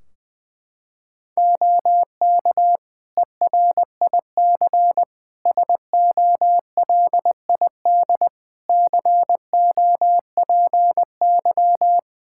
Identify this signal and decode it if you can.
Morse code, heard as a keyed tone, about 20 wpm, 700 Hz. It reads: OK ERIC SOLID COPY
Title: Practice QSO - Exchange 3 - More Info - 20wpm